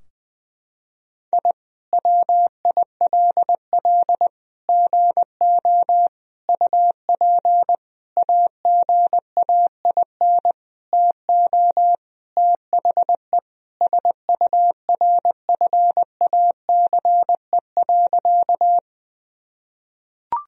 Morse code, heard as a keyed tone, about 20 wpm, 700 Hz. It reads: I WILL GO UP AGAIN TO THE SURFACE.
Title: Sentences from Top 1000 Words - Review Part 3 - 20wpm